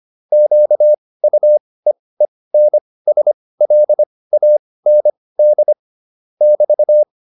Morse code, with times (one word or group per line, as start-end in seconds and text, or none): 0.3-5.7 QUEENSLAND
6.4-7.0 <BT>